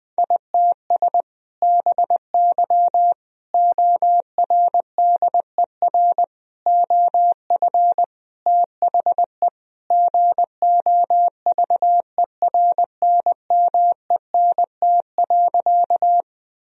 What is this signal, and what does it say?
Morse code, heard as a keyed tone, about 20 wpm, 700 Hz. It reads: ITS BY ORDER OF THE GOVERNMENT.
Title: Sentences from Top 1000 Words - Review Part 3 - 20wpm